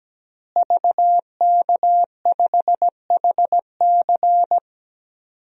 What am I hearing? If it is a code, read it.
Morse code, heard as a keyed tone, about 17 wpm, 700 Hz. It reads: VK5HC
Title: DXCC-100 Call Signs - 5 Characters - Repeated 3x - 17wpm